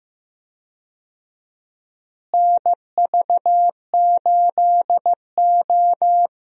Morse code, heard as a keyed tone, about 15 wpm, 700 Hz. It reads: NV8O
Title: USA Call Signs - 4 Characters - Repeated 3x - 15wpm